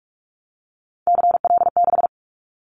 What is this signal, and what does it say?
Morse code, heard as a keyed tone, about 44 wpm, 700 Hz. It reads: CL6
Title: Callsign Prefixes for Top 100 DXCC - 45wpm